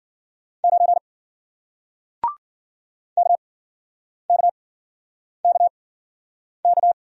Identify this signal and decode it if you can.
Morse code, heard as a keyed tone, about 52 wpm, spaced as for 9 wpm, 700 Hz. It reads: 9 KKKK